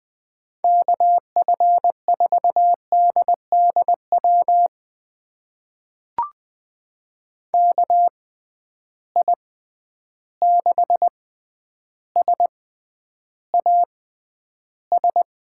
Morse code, heard as a keyed tone, about 20 wpm, spaced as for 10 wpm, 700 Hz. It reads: KF4DDW KI6SAS
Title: USA Call Signs - 6 Characters - 6x Character Spacing - 20wpm